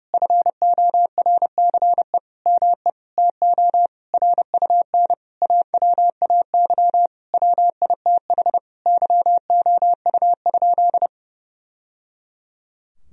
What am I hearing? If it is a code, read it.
Morse code, heard as a keyed tone, about 30 wpm, 700 Hz. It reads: FORCE ME TO RUN AWAY WITH YOU?
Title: Sentences from Top 1000 Words - Review Part 3 - 30wpm